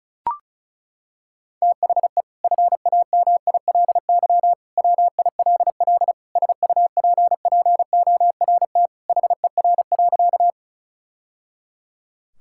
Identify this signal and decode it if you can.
Morse code, heard as a keyed tone, about 36 wpm, 700 Hz. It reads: THE FAMILY WILL SUPPORT HER.
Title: Sentences from Top 1000 Words - Review Part 5 - 35wpm